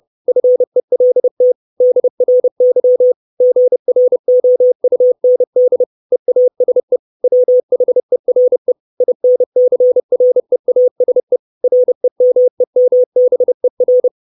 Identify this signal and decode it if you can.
Morse code, heard as a keyed tone, about 30 wpm, 500 Hz. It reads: FELT DRY GROUND EASE WHERE INCREASE REMEMBER
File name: 632 Words 30WPM